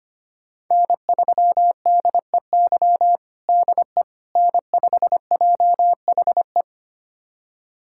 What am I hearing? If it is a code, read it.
Morse code, heard as a keyed tone, about 25 wpm, 700 Hz. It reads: N3DEY DE N5JHE
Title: USA Callsign DE Callsign - 5 Characters - 25wpm